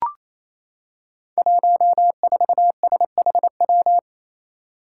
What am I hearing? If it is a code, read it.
Morse code, heard as a keyed tone, about 27 wpm, 700 Hz. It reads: 14SHW
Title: Five Challenging Letter-Number Combinations 28wpm